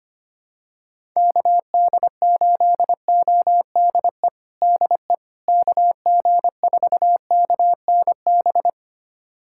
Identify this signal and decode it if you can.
Morse code, heard as a keyed tone, about 25 wpm, 700 Hz. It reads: KD8ODE DE KG4KNB
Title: USA Callsign DE Callsign - 6 Characters - 25wpm